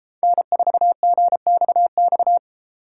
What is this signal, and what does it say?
Morse code, heard as a keyed tone, about 31 wpm, 700 Hz. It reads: N4GXX